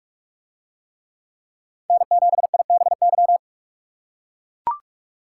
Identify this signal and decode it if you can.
Morse code, heard as a keyed tone, about 43 wpm, 700 Hz. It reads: N7IBY